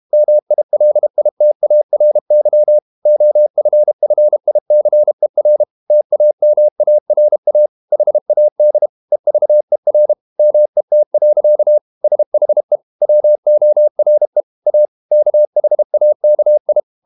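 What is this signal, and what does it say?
Morse code, heard as a keyed tone, about 32 wpm, 600 Hz. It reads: MILITARY OFFICER TAMARA HAD EVER MET. SHE WORE A KHAKI
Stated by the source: Morse code practice at 32 wpm from CW med Gnister, reading Ken Follett's novel Never in English.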